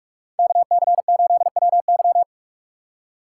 Morse code, heard as a keyed tone, about 44 wpm, 700 Hz. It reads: KC8WY